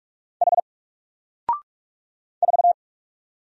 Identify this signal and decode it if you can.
Morse code, heard as a keyed tone, about 46 wpm, 700 Hz. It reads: H 4
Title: Single Character (Letter:Number:Punctuation:Prosign) 45wpm